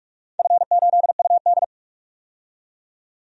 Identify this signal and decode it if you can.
Morse code, heard as a keyed tone, about 46 wpm, 700 Hz. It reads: F8UD
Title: DXCC-100 Call Signs - 4 Characters - 45wpm